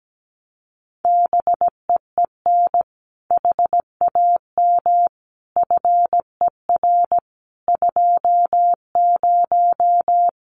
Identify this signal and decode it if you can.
Morse code, heard as a keyed tone, about 17 wpm, 700 Hz. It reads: BEEN HAM FER 20
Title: Common QSO Phrases 17wpm